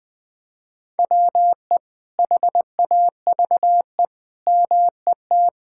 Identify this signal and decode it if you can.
Morse code, heard as a keyed tone, about 21 wpm, 700 Hz. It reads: WE HAVE MET
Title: Sets of 9 Words - 20wpm